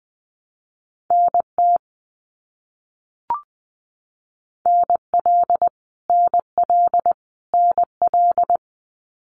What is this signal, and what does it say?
Morse code, heard as a keyed tone, about 20 wpm, 700 Hz. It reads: NT NL NL NL
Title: Canadian Provinces and Territories - Repeated 3x - 20wpm